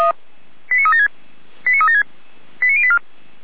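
Five-tone selective call (ZVEI), heard as quick runs of five tones, tones of about 70 ms. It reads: 89376 89376 89084
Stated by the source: ZVEI five-tone selective call received on SDR, Feuerwehr Orlamünde.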